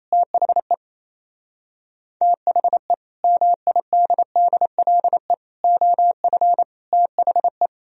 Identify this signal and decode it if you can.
Morse code, heard as a keyed tone, about 35 wpm, 700 Hz. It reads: THE THE MIDDLE OF THE
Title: Speed-Racing - Sets of 4 Words 28wpm